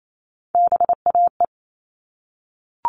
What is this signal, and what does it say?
Morse code, heard as a keyed tone, about 28 wpm, 700 Hz. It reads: BAE
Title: All Cut Numbers - 3-Digit 28wpm